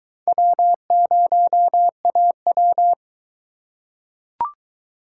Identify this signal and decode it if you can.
Morse code, heard as a keyed tone, about 21 wpm, 700 Hz. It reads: W0AW